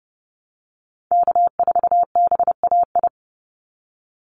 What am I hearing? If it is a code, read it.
Morse code, heard as a keyed tone, about 30 wpm, 700 Hz. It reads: K4BAI
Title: Call Signs Repeated 3x - 30wpm